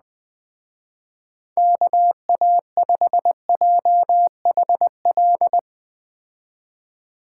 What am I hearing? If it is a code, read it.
Morse code, heard as a keyed tone, about 20 wpm, 700 Hz. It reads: KA5JHL
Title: Speed Racing - USA 2x3 Call Signs 17wpm